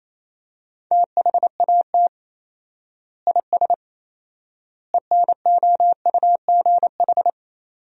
Morse code, heard as a keyed tone, about 28 wpm, 700 Hz. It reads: THAT IS ENOUGH
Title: 4x Word-Spacing - Sentences from Top 400 Words - 28wpm